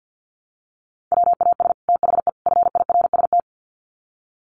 Morse code, heard as a keyed tone, about 50 wpm, 700 Hz. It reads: WAS THE FIRST
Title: Sets of 3 Words 50wpm